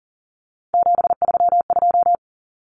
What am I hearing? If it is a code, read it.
Morse code, heard as a keyed tone, about 37 wpm, 700 Hz. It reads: 732
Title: Common Cut Numbers - 3-Digit 40wpm